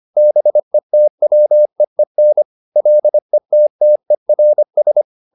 Morse code, heard as a keyed tone, about 25 wpm, 600 Hz. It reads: BETWEEN LETTERS